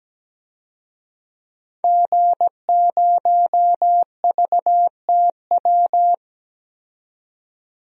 Morse code, heard as a keyed tone, about 17 wpm, 700 Hz. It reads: G0VTW
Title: DXCC-100 Call Signs - 5 Characters - 17wpm